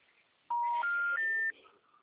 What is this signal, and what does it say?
Special information tone.